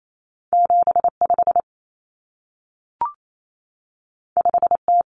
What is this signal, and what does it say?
Morse code, heard as a keyed tone, about 28 wpm, 700 Hz. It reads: 75 5T
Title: Common Cut Numbers - 2-Digit 28wpm